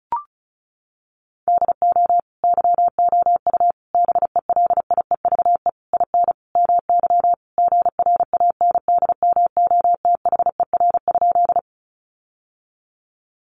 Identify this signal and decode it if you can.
Morse code, heard as a keyed tone, about 34 wpm, 700 Hz. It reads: DO YOU BELIEVE IN MY GRANDMOTHER?
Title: Sentences from Top 1200 Words 35wpm